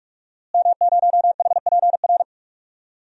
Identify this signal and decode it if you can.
Morse code, heard as a keyed tone, about 48 wpm, 700 Hz. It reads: M0HPR
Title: DXCC-100 Call Signs - 5 Characters - Repeated 2x - 45wpm